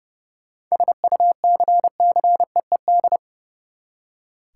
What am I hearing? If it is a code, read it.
Morse code, heard as a keyed tone, about 31 wpm, 700 Hz. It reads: SUCCEED
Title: Seven Letter Words - 30wpm